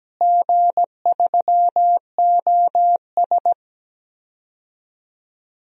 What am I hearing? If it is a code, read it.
Morse code, heard as a keyed tone, about 17 wpm, 700 Hz. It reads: G3OS
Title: DXCC-100 Call Signs - 4 Characters - Repeated 2x - 17wpm